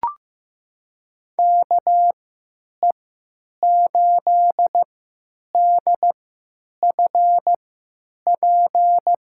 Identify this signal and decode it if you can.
Morse code, heard as a keyed tone, about 15 wpm, spaced as for 9 wpm, 700 Hz. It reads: KE8DFP
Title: USA Call Signs - 6 Characters - 3x Character Spacing - 15wpm